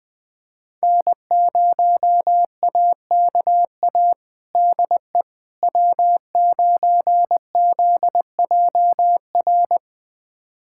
Morse code, heard as a keyed tone, about 21 wpm, 700 Hz. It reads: N0AKA DE W9ZJR
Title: USA Callsign DE Callsign - 5 Characters - 20wpm